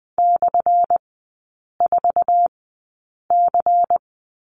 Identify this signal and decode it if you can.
Morse code, heard as a keyed tone, about 20 wpm, spaced as for 8 wpm, 690 Hz. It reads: /4C
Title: 198 Lesson 32e - Letter k - Three Characters 20wpm